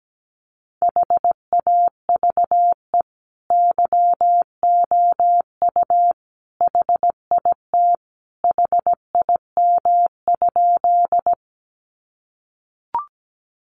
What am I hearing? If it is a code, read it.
Morse code, heard as a keyed tone, about 17 wpm, 700 Hz. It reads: HAVE YOU HIT HIM?
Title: Sentences from Top 1200 Words 17wpm